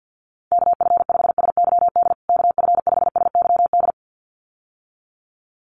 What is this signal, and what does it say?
Morse code, heard as a keyed tone, about 50 wpm, 700 Hz. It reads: KF5SYD KF5SYD
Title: USA Call Signs Repeated 2x 50wpm